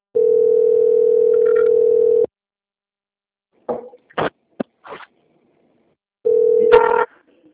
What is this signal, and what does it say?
Ringback tone (North America), 440 + 480 Hz.